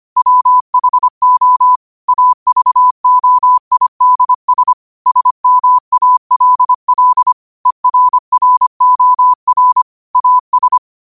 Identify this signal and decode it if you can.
Morse code, heard as a keyed tone, about 25 wpm, 1000 Hz. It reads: WHO AVOIDS SMALL ERROR AS
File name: qotdfiles / 25wpm